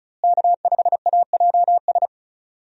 Morse code, heard as a keyed tone, about 34 wpm, 700 Hz. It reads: K5AJS
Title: USA Call Signs - 5 Characters - Repeated 3x - 35wpm